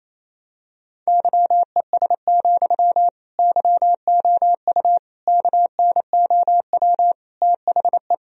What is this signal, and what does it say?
Morse code, heard as a keyed tone, about 28 wpm, 700 Hz. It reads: YES, YOU KNOW THE
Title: Sentences from Top 1000 Words - Review Part 4 - 28wpm